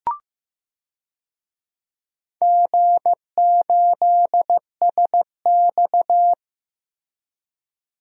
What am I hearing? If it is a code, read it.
Morse code, heard as a keyed tone, about 15 wpm, 700 Hz. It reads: G8SX
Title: DXCC-100 Call Signs - 4 Characters - Repeated 2x - 15wpm